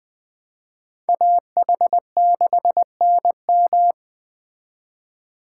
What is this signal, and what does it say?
Morse code, heard as a keyed tone, about 20 wpm, 700 Hz. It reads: AH6NM
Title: Speed Racing - USA 2x2 Call Signs 17wpm